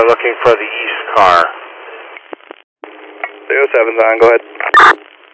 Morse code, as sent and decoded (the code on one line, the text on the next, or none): none